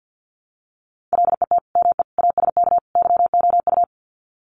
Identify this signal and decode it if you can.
Morse code, heard as a keyed tone, about 50 wpm, 700 Hz. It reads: LET ME ASK YOU
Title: Sets of 4 Words 50wpm